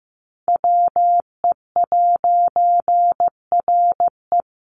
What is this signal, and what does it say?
Morse code, heard as a keyed tone, about 15 wpm, 700 Hz. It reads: WE'RE
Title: A Sound Of Thunder 15wpm